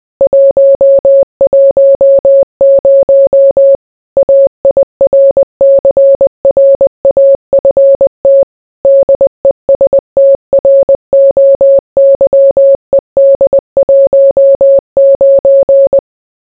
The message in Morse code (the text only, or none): 110 AIRCRAFT DESTROYED 19